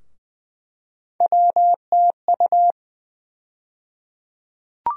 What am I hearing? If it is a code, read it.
Morse code, heard as a keyed tone, about 20 wpm, 700 Hz. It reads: WTU
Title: USA Call Sign Postfixes - 3 Letters 20wpm